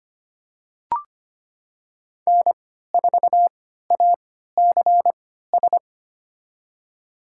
Morse code, heard as a keyed tone, about 25 wpm, spaced as for 14 wpm, 700 Hz. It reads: N4ACS
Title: USA Call Signs - 5 Characters - 3x Character Spacing - 25wpm